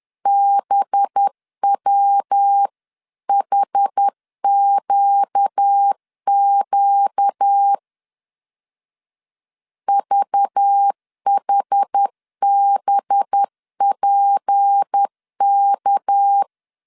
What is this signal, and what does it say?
Morse code, heard as a keyed tone, about 10 wpm, 800 Hz. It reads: BWHQQ VHBPK
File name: CW / DET / 1 6 1500C S B 10wpm 4137